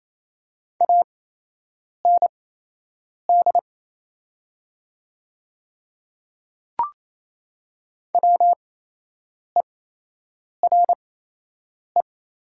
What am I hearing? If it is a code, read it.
Morse code, heard as a keyed tone, about 28 wpm, spaced as for 8 wpm, 700 Hz. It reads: AND WERE
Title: 8x Character Spacing - Top 100 Words - 28wpm